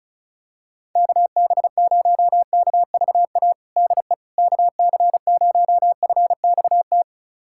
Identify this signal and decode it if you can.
Morse code, heard as a keyed tone, about 36 wpm, 700 Hz. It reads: KB0KVA DE KC0FXT
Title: USA Callsign DE Callsign - 6 Characters - 35wpm